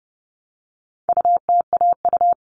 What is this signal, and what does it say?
Morse code, heard as a keyed tone, about 30 wpm, 700 Hz. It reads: UTAU
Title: All Cut Numbers - 4-Digit 30wpm